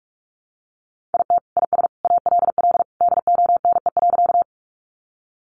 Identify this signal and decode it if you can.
Morse code, heard as a keyed tone, about 44 wpm, 700 Hz. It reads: IT IS ALL DONE.
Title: Sentences from Top 500 Words 45wpm